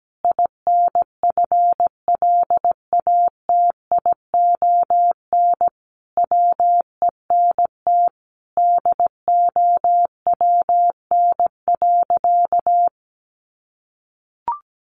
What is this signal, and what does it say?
Morse code, heard as a keyed tone, about 17 wpm, 700 Hz. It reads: INFLATION WENT DOWN.
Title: Sentences from Top 1100 Words 17wpm